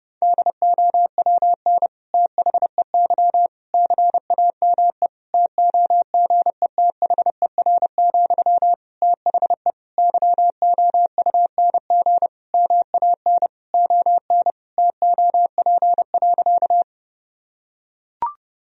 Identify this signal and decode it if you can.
Morse code, heard as a keyed tone, about 32 wpm, 700 Hz. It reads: DOWN THEY CAME TOGETHER, THE YOUNG MAN ON TOP.
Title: Sentences from Top 1000 Words - Review Part 2 - 30wpm